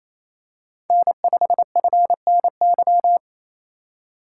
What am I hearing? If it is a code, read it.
Morse code, heard as a keyed tone, about 27 wpm, 700 Hz. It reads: N5FNY
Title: USA Call Signs - 3x Character Spacing - 28wpm